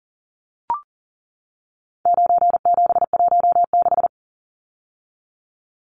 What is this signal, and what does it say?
Morse code, heard as a keyed tone, about 39 wpm, 700 Hz. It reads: 9716
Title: Numbers - 4-Digit 40wpm